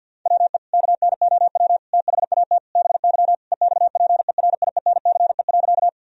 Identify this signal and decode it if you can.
Morse code, heard as a keyed tone, about 50 wpm, 700 Hz. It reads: WE KNOW THAT BY EXPERIENCE.
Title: Sentences from Top 1000 Words - Review Part 3 - 50wpm